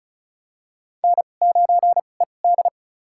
Morse code, heard as a keyed tone, about 34 wpm, spaced as for 25 wpm, 700 Hz. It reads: N9ED
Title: USA Call Signs - 2x Character Spacing - 35wpm